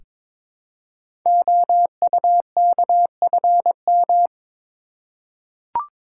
Morse code, heard as a keyed tone, about 22 wpm, 700 Hz. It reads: OUKFM
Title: Code Groups - 5 Letters - 22wpm